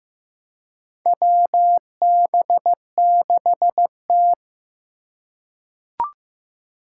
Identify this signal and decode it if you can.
Morse code, heard as a keyed tone, about 15 wpm, 700 Hz. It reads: WB6T
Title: USA Call Signs - 2x Character Spacing - 15wpm